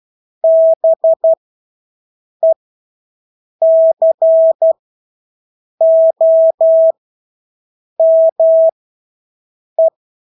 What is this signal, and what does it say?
Morse code, heard as a keyed tone, about 12 wpm, spaced as for 6 wpm, 650 Hz. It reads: BECOME
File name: Engelske ord 12-6 wpm